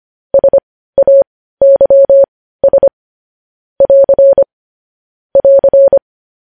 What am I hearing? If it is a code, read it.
Morse code, heard as a keyed tone, about 25 wpm, spaced as for 15 wpm, 550 Hz. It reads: SAYS <AR> <AR>
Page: news-headlines-in-morse-code-at-15-wpm